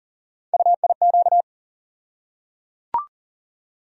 Morse code, heard as a keyed tone, about 38 wpm, 700 Hz. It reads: UIQ
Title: USA Call Sign Postfixes - 3 Letters 40wpm